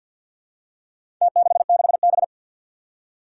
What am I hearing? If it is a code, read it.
Morse code, heard as a keyed tone, about 52 wpm, 700 Hz. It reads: T66B
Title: Four Challenging Letter-Number Combinations 50wpm